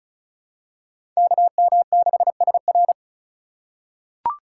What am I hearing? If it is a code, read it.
Morse code, heard as a keyed tone, about 36 wpm, 700 Hz. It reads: KM6SR
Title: USA Call Signs - 5 Characters - 2x Character Spacing - 35wpm